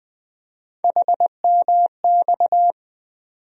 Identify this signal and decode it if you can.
Morse code, heard as a keyed tone, about 20 wpm, 700 Hz. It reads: HMX
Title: USA Call Sign Postfixes - 3 Letters 20wpm